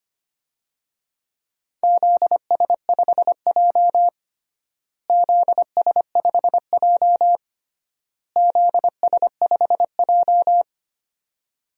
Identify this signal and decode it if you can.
Morse code, heard as a keyed tone, about 25 wpm, 700 Hz. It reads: ZS5J ZS5J ZS5J
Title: DXCC-100 Call Signs - 4 Characters - Repeated 3x - 25wpm